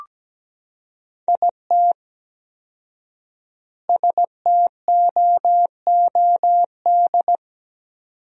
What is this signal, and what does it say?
Morse code, heard as a keyed tone, about 17 wpm, 700 Hz. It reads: IT STOOD
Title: 4x Word-Spacing - Sentences from Top 500 Words - 17wpm